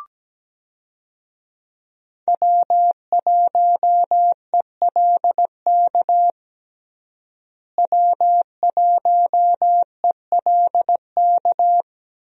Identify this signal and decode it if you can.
Morse code, heard as a keyed tone, about 17 wpm, 700 Hz. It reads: W1ELK W1ELK
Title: USA Call Signs - 5 Characters - Repeated 3x - 17wpm